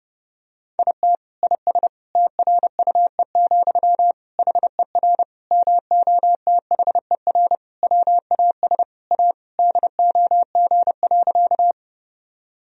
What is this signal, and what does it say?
Morse code, heard as a keyed tone, about 30 wpm, 700 Hz. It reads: IT IS TRUE, HER MOTHER WAS A DOG.
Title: Sentences from Top 1000 Words - Review Part 2 - 30wpm